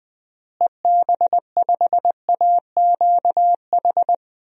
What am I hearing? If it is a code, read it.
Morse code, heard as a keyed tone, about 20 wpm, 700 Hz. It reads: EB5AQH
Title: DXCC-100 Call Signs - 6 Characters - Repeated 3x - 20wpm